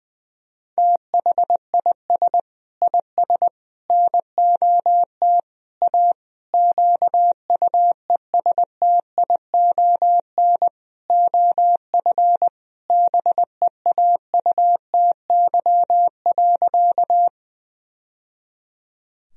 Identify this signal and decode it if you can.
Morse code, heard as a keyed tone, about 21 wpm, 700 Hz. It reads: THIS IS NOT A QUESTION OF BEAUTY.
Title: Sentences from Top 1000 Words - Review Part 2 - 20wpm